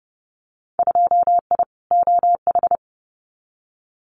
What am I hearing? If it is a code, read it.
Morse code, heard as a keyed tone, about 30 wpm, 700 Hz. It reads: 2I OH
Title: Winter Field Day Exchanges 30wpm